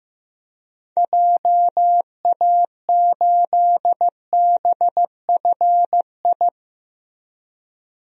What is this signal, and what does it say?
Morse code, heard as a keyed tone, about 15 wpm, 700 Hz. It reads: JA8BFI
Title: DXCC-100 Call Signs - 6 Characters - Repeated 3x - 15wpm